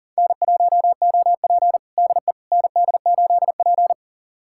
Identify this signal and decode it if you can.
Morse code, heard as a keyed tone, about 37 wpm, 700 Hz. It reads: N1OP DE ND8P
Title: USA Callsign DE Callsign - 4 Characters - 40wpm